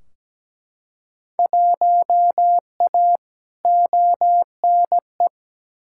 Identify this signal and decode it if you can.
Morse code, heard as a keyed tone, about 17 wpm, 700 Hz. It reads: 1A ONE
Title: Field Day Exchanges Encore - 17wpm